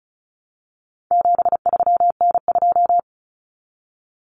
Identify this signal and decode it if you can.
Morse code, heard as a keyed tone, about 34 wpm, 700 Hz. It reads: 73N2
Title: Common Cut Numbers - 4-Digit 35wpm